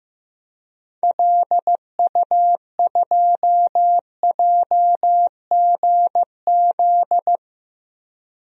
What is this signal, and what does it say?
Morse code, heard as a keyed tone, about 15 wpm, 700 Hz. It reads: LU2JGZ
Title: DXCC-100 Call Signs - 6 Characters - 15wpm